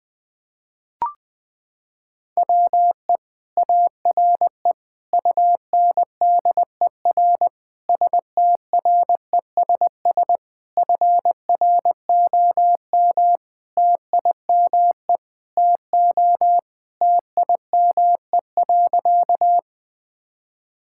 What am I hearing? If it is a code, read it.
Morse code, heard as a keyed tone, about 20 wpm, 700 Hz. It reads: WE ARE UNDER STRESS FROM TIME TO TIME.
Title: Sentences from Top 1000 Words - Review Part 6 - 20wpm